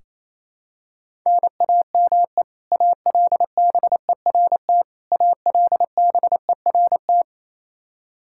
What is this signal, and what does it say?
Morse code, heard as a keyed tone, about 28 wpm, 700 Hz. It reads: NAME ALBERT ALBERT
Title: Practice QSO - Exchange 1 - Beginning - 28wpm